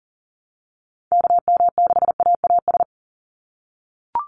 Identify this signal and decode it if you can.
Morse code, heard as a keyed tone, about 40 wpm, 700 Hz. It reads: KM6AAS
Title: USA Call Signs 40wpm